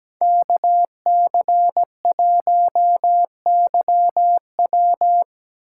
Morse code, heard as a keyed tone, about 18 wpm, 700 Hz. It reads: KC1YW
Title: Speed Racing - USA Call Signs 17wpm